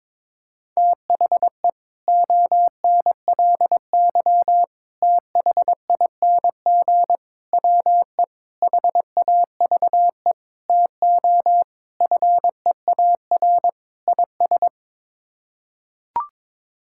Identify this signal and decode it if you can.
Morse code, heard as a keyed tone, about 22 wpm, 700 Hz. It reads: THE ONLY THING WE HAVE TO FEAR IS
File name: practice / 22wpm / Sets of 8 Words - 22wpm